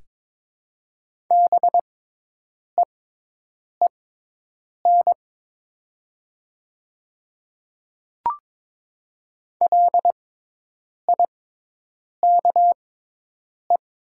Morse code, heard as a keyed tone, about 22 wpm, spaced as for 8 wpm, 700 Hz. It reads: BEEN LIKE
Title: 6x Character Spacing - Top 200 Words - 22wpm